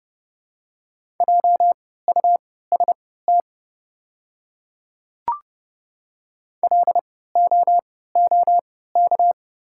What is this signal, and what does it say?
Morse code, heard as a keyed tone, about 30 wpm, spaced as for 17 wpm, 700 Hz. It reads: JUST LOOK
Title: 3x Character Spacing - Top 200 Words - 30wpm